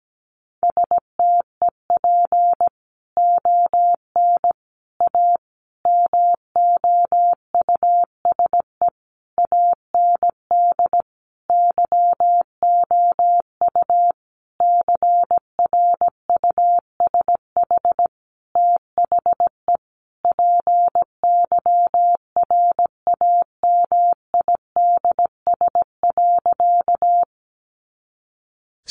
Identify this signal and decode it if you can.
Morse code, heard as a keyed tone, about 17 wpm, 700 Hz. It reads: STEP ON A MOUSE AND YOU CRUSH THE PYRAMIDS.
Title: A Sound Of Thunder 17wpm